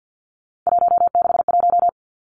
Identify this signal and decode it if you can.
Morse code, heard as a keyed tone, about 50 wpm, 700 Hz. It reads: 161
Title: Three Challenging Letter-Number Combinations 50wpm